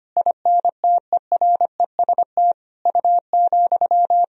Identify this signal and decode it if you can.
Morse code, heard as a keyed tone, about 25 wpm, 700 Hz. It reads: INTEREST U,
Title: Sentences from Top 400 Words and Abbreviations 25wpm